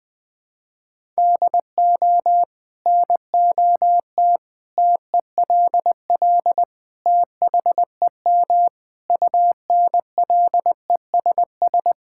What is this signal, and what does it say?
Morse code, heard as a keyed tone, about 20 wpm, 700 Hz. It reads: DO NOT TELL THEM UNLESS
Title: Sentences from Top 1000 Words - Review Part 6 - 20wpm